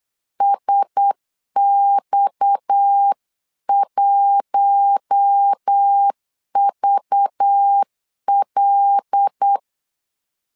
Morse code, heard as a keyed tone, about 8 wpm, 800 Hz. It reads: SX1VL
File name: LEZIONI / 7 GF9Z / 7 G 8wpm 3118